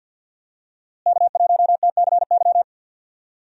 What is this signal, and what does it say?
Morse code, heard as a keyed tone, about 52 wpm, 700 Hz. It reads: K1TCY
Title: USA Call Signs - 5 Characters - Repeated 3x - 50wpm